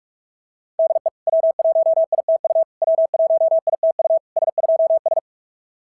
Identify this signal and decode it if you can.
Morse code, heard as a keyed tone, about 44 wpm, 660 Hz. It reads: DE W1ITU W1ITU S2S
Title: SOTA QSO Practice 45wpm